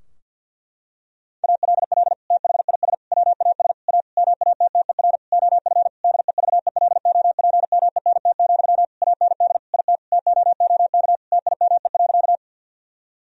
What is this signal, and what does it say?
Morse code, heard as a keyed tone, about 50 wpm, 700 Hz. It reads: ALL THIS WAS A MATTER OF DEVELOPMENT, AND IT TOOK TIME.
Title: Sentences from Top 1000 Words - Review Part 4 - 50wpm